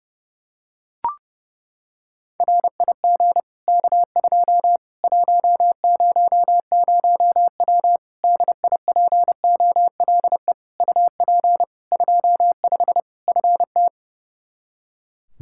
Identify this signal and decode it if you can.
Morse code, heard as a keyed tone, about 29 wpm, 700 Hz. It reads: RIG K2 100W DIPOLE UP 25 FT
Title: Practice QSO - Exchange 2 - Info - 30wpm